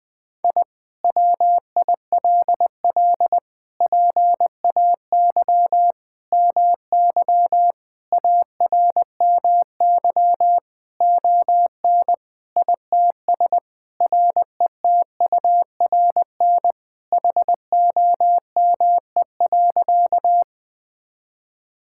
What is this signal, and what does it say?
Morse code, heard as a keyed tone, about 21 wpm, 700 Hz. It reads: I WILL PAY MY ARMY ON ITS RETURN HOME.
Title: Sentences from Top 1000 Words - Review Part 5 - 20wpm